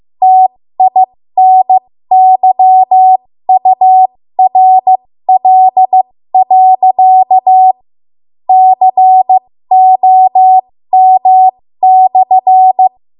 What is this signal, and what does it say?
Morse code, heard as a keyed tone, about 15 wpm, spaced as for 12 wpm, 750 Hz. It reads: TINYURL.COM/
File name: Morse / 13WPM